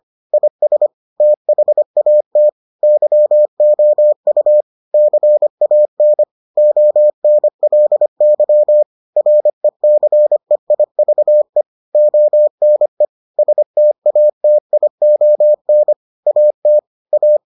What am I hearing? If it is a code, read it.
Morse code, heard as a keyed tone, about 25 wpm, 600 Hz. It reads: IS THAT YOU CAN ONLY RECEIVE ONE STATION AT A